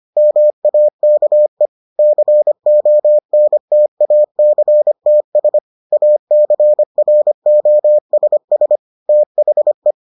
Morse code, heard as a keyed tone, about 25 wpm, 600 Hz. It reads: MAKE CONTACTS ACROSS THE